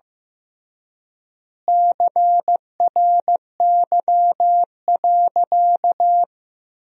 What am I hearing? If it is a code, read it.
Morse code, heard as a keyed tone, about 15 wpm, 700 Hz. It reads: CRY.
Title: 3x Word-Spacing - Sentences from Top 500 Words - 15wpm